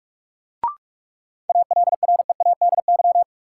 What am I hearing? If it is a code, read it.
Morse code, heard as a keyed tone, about 46 wpm, 700 Hz. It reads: ALREADY
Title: Sentences from Top 700 Words - New Word Drill - 45wpm